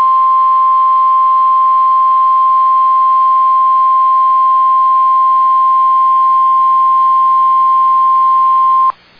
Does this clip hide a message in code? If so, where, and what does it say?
Yes: Emergency Alert System (SAME) tones (the 1050 Hz attention tone) from 0.9 to 8.8 s.